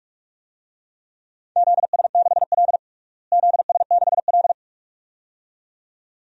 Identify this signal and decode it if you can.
Morse code, heard as a keyed tone, about 42 wpm, 700 Hz. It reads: ZS6L ZS6L
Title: DXCC-100 Call Signs - 4 Characters - Repeated 2x - 45wpm